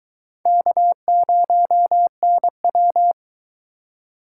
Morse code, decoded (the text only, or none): K0NW